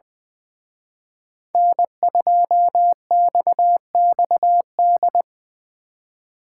Morse code, heard as a keyed tone, about 24 wpm, 700 Hz. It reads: N2XXD
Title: Speed Racing - USA 1x3 Call Signs 20wpm